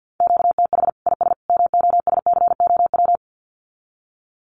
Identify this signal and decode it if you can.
Morse code, heard as a keyed tone, about 48 wpm, 700 Hz. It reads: QTH IS MOSCOW
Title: Common QSO Phrases 50wpm